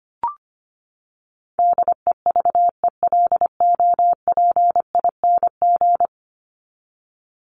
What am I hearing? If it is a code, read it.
Morse code, heard as a keyed tone, about 25 wpm, 700 Hz. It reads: DEVELOPING
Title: ING Suffix - Top 100 Words with Base Words 25wpm